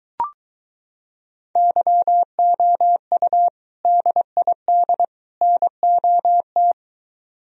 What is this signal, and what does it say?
Morse code, heard as a keyed tone, about 23 wpm, 700 Hz. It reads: YOU DID NOT